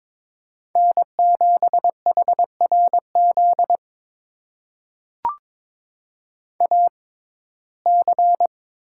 Morse code, heard as a keyed tone, about 22 wpm, spaced as for 16 wpm, 700 Hz. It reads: N7HRZ AC
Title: USA Call Signs - 5 Characters - 6x Character Spacing - 22wpm